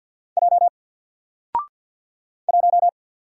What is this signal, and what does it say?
Morse code, heard as a keyed tone, about 50 wpm, 700 Hz. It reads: J 1
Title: Single Letter-Number Alt 50wpm